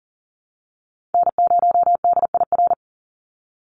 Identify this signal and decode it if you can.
Morse code, heard as a keyed tone, about 40 wpm, 700 Hz. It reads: N0DIR
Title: Call Signs Repeated 2x - 40wpm